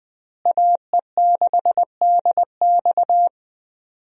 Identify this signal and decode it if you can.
Morse code, heard as a keyed tone, about 20 wpm, 700 Hz. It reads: AE6DX
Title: USA Call Signs - 5 Characters - Repeated 3x - 20wpm